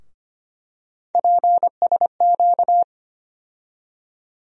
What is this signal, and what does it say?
Morse code, heard as a keyed tone, about 25 wpm, 700 Hz. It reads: PSQ
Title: USA Call Sign Postfixes - 3 Letters 25wpm